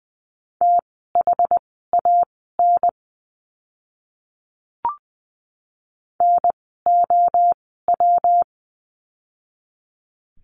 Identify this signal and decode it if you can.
Morse code, heard as a keyed tone, about 20 wpm, spaced as for 14 wpm, 700 Hz. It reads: THAN NOW
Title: 185 Lesson 30f - Letter b - Words 20wpm